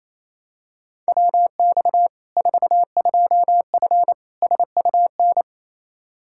Morse code, heard as a keyed tone, about 28 wpm, 700 Hz. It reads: WX 42F SUN
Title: K1USN SST QSO Practice 28wpm